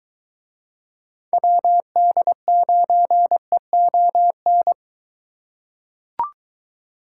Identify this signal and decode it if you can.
Morse code, heard as a keyed tone, about 23 wpm, 700 Hz. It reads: WD9EON